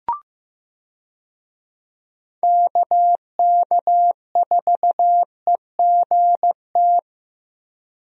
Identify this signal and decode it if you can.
Morse code, heard as a keyed tone, about 15 wpm, 700 Hz. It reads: KK4EGT
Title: USA Call Signs - 6 Characters - Repeated 3x - 15wpm